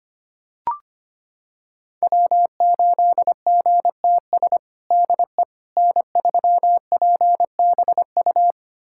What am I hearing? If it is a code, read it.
Morse code, heard as a keyed tone, about 25 wpm, 700 Hz. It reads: W8GTS DE N3PBU
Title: USA Callsign DE Callsign - 5 Characters - 25wpm